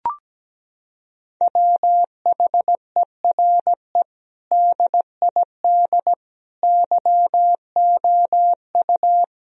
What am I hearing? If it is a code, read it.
Morse code, heard as a keyed tone, about 17 wpm, 700 Hz. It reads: WHERE DID YOU
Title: Sentences from Top 1000 Words - Review Part 3 - 17wpm